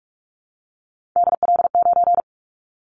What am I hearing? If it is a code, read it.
Morse code, heard as a keyed tone, about 44 wpm, 700 Hz. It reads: DL9
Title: Callsign Prefixes for Top 25 DXCC - 45wpm